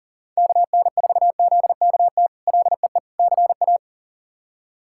Morse code, heard as a keyed tone, about 40 wpm, 700 Hz. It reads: KN4ZKT LEE CA
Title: K1USN SST QSO Practice - 40wpm